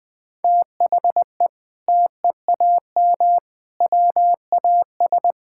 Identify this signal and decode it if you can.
Morse code, heard as a keyed tone, about 20 wpm, 700 Hz. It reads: THE TEAM WAS
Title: Sentences with 6-Word Phrases - 20wpm